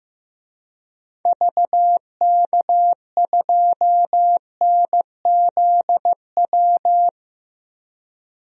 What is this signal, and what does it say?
Morse code, heard as a keyed tone, about 15 wpm, 700 Hz. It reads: VK2NZW
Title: DXCC-100 Call Signs - 6 Characters - 15wpm